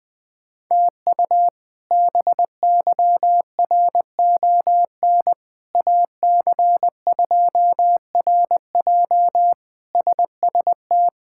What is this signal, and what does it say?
Morse code, heard as a keyed tone, about 20 wpm, 700 Hz. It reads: TU BYRON AC2RJ SST
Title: K1USN SST QSO Practice - 20wpm